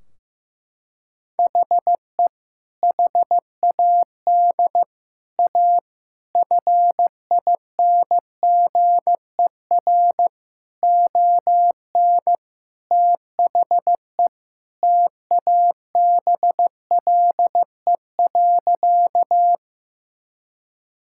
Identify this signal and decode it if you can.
Morse code, heard as a keyed tone, about 15 wpm, 700 Hz. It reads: HE HAD A FINGER ON THE TABLE.
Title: Sentences from Top 1600 Words 15wpm